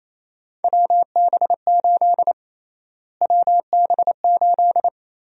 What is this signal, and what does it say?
Morse code, heard as a keyed tone, about 28 wpm, 700 Hz. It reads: WB8 WB8
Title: USA Call Sign Prefixes - Repeated 2x - 28wpm